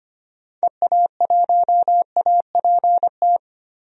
Morse code, heard as a keyed tone, about 25 wpm, 700 Hz. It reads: EA1APT
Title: DXCC-100 Call Signs - 6 Characters - Repeated 3x - 25wpm